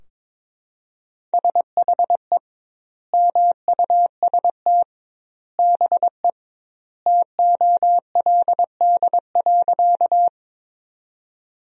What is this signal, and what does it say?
Morse code, heard as a keyed tone, about 22 wpm, 700 Hz. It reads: SHE MUST BE TOLD.
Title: 2x Word-Spacing - Sentences from Top 400 Words - 22wpm